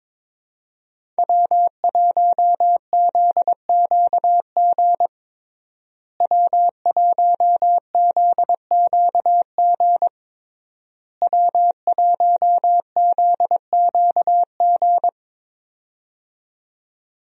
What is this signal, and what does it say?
Morse code, heard as a keyed tone, about 22 wpm, 700 Hz. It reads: W1ZQG W1ZQG W1ZQG
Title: USA Call Signs - 5 Characters - Repeated 3x - 22wpm